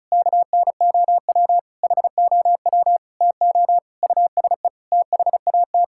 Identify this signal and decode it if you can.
Morse code, heard as a keyed tone, about 36 wpm, 700 Hz. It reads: KNOW HOW TO USE THAT
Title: Sentences from Top 1000 Words - Review Part 6 - 35wpm